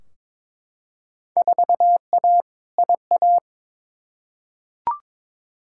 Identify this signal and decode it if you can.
Morse code, heard as a keyed tone, about 22 wpm, 700 Hz. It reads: 4A IA
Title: Field Day Exchanges - 22wpm